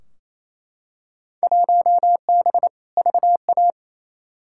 Morse code, heard as a keyed tone, about 28 wpm, 700 Hz. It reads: 1B VA
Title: Field Day Exchanges Encore - 28wpm